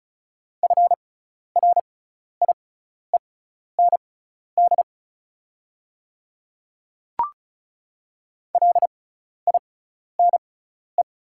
Morse code, heard as a keyed tone, about 36 wpm, spaced as for 12 wpm, 700 Hz. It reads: FRIEND LINE
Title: 6x Character Spacing - Top 300 Words - 35wpm